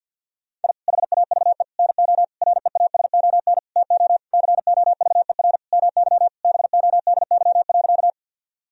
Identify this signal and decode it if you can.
Morse code, heard as a keyed tone, about 46 wpm, 700 Hz. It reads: I HAVE NO REASON TO COVER MY BODY.
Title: Sentences from Top 1000 Words - Review Part 1 - 50wpm